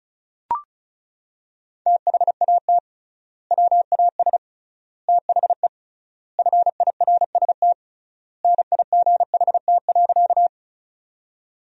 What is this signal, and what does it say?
Morse code, heard as a keyed tone, about 36 wpm, 700 Hz. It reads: THAT WAS THE FIRST NIGHT.
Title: 3x Word-Spacing - Sentences from Top 300 Words - 35wpm